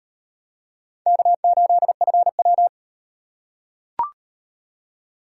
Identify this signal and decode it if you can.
Morse code, heard as a keyed tone, about 36 wpm, 700 Hz. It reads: K8FW